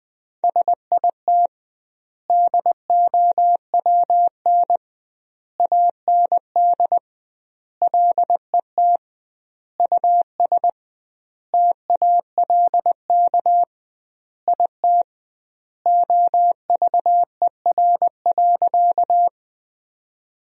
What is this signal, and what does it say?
Morse code, heard as a keyed tone, about 20 wpm, 700 Hz. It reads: SIT DOWN AND LET US TALK IT OVER.
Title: 2x Word-Spacing - Sentences from Top 500 Words - 20wpm